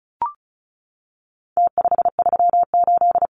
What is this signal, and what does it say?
Morse code, heard as a keyed tone, about 34 wpm, 700 Hz. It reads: T538
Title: Numbers - Contest Style - 4-Digit 35wpm